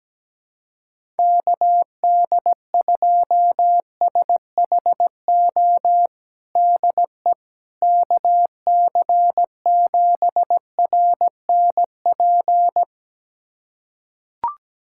Morse code, heard as a keyed tone, about 17 wpm, 700 Hz. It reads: KD2SHO DE KC7RNP
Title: USA Callsign DE Callsign - 6 Characters - 17wpm